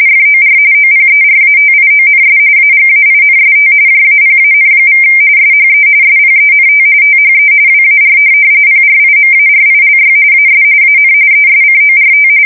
RTTY, about 45.45 baud. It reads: FOR OPERATIONS AGAINST JAPAN= SHE SPENT THE WAR PRIMARILY AS AN ESCORT
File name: fsk170 / USS-Massachusetts